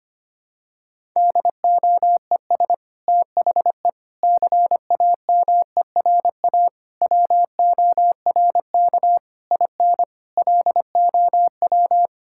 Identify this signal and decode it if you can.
Morse code, heard as a keyed tone, about 25 wpm, 700 Hz. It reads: DOES THE CAMERA WORK IN LOW
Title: Sentences from Top 1000 Words - Review Part 5 - 25wpm